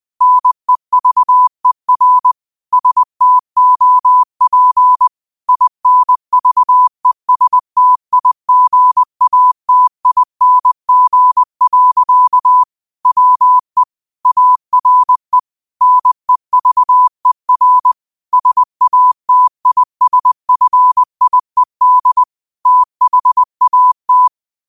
Morse code, heard as a keyed tone, about 20 wpm, 1000 Hz. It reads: NEVER STOP INVESTIGATING. WE ARE NEVER SATISFIED THAT
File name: 20wpm